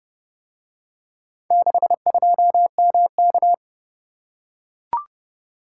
Morse code, heard as a keyed tone, about 30 wpm, 700 Hz. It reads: 62MK